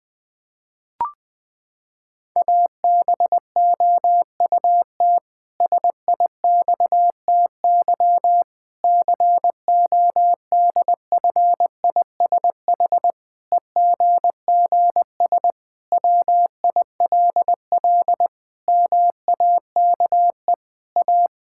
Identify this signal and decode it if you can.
Morse code, heard as a keyed tone, about 21 wpm, 700 Hz. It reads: ABOUT SIXTY CODFISH EGGS WILL MAKE A
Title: Pangram Challenge v2 - 20wpm